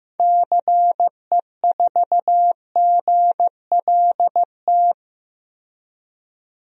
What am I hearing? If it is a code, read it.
Morse code, heard as a keyed tone, about 15 wpm, 700 Hz. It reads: CE4GLT
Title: DXCC-100 Call Signs - 6 Characters - 15wpm